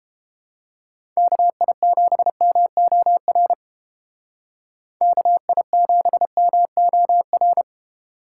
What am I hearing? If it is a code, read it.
Morse code, heard as a keyed tone, about 31 wpm, 700 Hz. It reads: KI7MOR KI7MOR